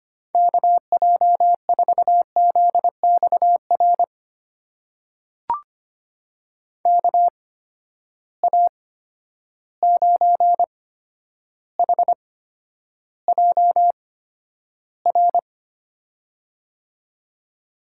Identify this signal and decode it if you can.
Morse code, heard as a keyed tone, about 25 wpm, spaced as for 11 wpm, 700 Hz. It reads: KJ4ZXR KA9HJR
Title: USA Call Signs - 6 Characters - 8x Character Spacing - 25wpm